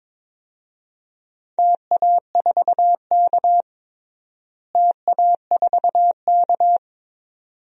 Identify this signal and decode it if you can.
Morse code, heard as a keyed tone, about 22 wpm, 700 Hz. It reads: TA4K TA4K
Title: DXCC-100 Call Signs - 4 Characters - Repeated 2x - 22wpm